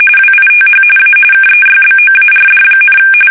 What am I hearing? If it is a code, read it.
RTTY, about 45.45 baud. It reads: WITH A KEYBOARD,
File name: fsk850 / teletype-corp